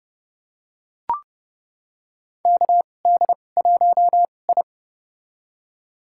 Morse code, heard as a keyed tone, about 30 wpm, spaced as for 22 wpm, 690 Hz. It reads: KD1I
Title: USA Call Signs - 4 Characters - 2x Character Spacing - 30wpm